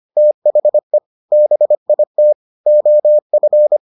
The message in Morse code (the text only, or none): THE BIT OF